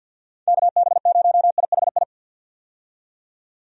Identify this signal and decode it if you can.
Morse code, heard as a keyed tone, about 51 wpm, 700 Hz. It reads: KB0IHI
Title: USA Call Signs - 6 Characters - 50wpm